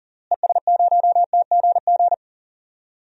Morse code, heard as a keyed tone, about 40 wpm, 700 Hz. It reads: ES0TGG
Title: DXCC-100 Call Signs - 6 Characters - Repeated 3x - 40wpm